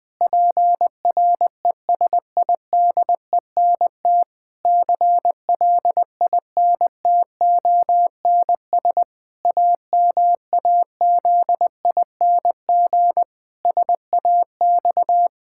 Morse code, heard as a keyed tone, about 21 wpm, 700 Hz. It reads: PRESIDENT CLINTONS AMAZING SAX
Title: Pangram Challenge v2 - 20wpm